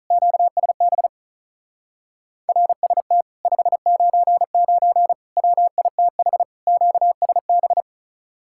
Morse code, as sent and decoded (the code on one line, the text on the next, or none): --.- ... -... / .-. ... - / ..... ----. ----. / .-- .. - .... / --.- ... -...
QSB RST 599 WITH QSB